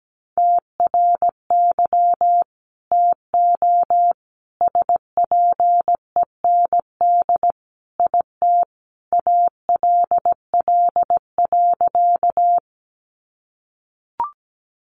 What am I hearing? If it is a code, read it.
Morse code, heard as a keyed tone, about 17 wpm, 700 Hz. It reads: TRY TO SPEND IT ALL.
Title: Sentences from Top 800 Words 17wpm